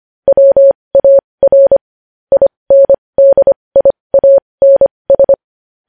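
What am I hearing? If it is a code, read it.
Morse code, heard as a keyed tone, about 25 wpm, spaced as for 20 wpm, 550 Hz. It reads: WAR INDIANS
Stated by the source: Morse code news headlines sent at 20.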